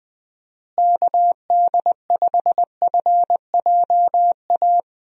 Morse code, heard as a keyed tone, about 20 wpm, 700 Hz. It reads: KD5FJA
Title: USA Call Signs - 6 Characters - Repeated 2x - 20wpm